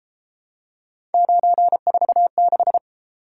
Morse code, heard as a keyed tone, about 30 wpm, 700 Hz. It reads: 946